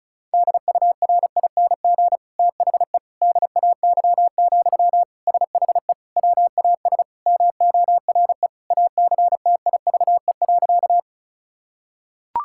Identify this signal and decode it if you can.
Morse code, heard as a keyed tone, about 34 wpm, 700 Hz. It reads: DURING THE DAY, SHE WAS MORE ACTIVE.
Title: Sentences from Top 1000 Words - Review Part 6 - 35wpm